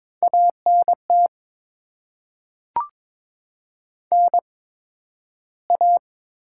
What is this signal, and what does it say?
Morse code, heard as a keyed tone, about 22 wpm, spaced as for 12 wpm, 700 Hz. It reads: ANT NA
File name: Top 20 QSO Elements - 8x Character Spacing - 22wpm